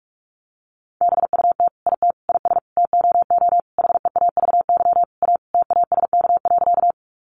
Morse code, heard as a keyed tone, about 44 wpm, 700 Hz. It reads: BUT IT IS TOO HEAVY A TASK.
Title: Sentences from Top 1100 Words 45wpm